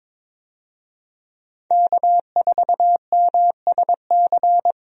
Morse code, heard as a keyed tone, about 22 wpm, 700 Hz. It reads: K4MSC
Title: USA Call Signs - 5 Characters - 22wpm